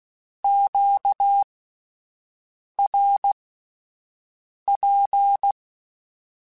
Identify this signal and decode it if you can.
Morse code, heard as a keyed tone, about 16 wpm, spaced as for 5 wpm, 800 Hz. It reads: QRP